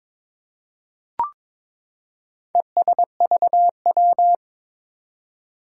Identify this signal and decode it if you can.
Morse code, heard as a keyed tone, about 21 wpm, 700 Hz. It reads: ESVW
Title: Four Challenging Letter-Number Combinations 22wpm